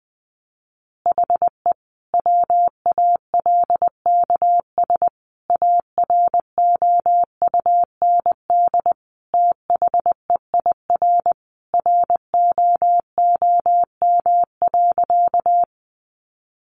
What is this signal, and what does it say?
Morse code, heard as a keyed tone, about 20 wpm, 700 Hz. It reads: HE WALKS AROUND THEIR ROOM.
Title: Easy Phrases 20wpm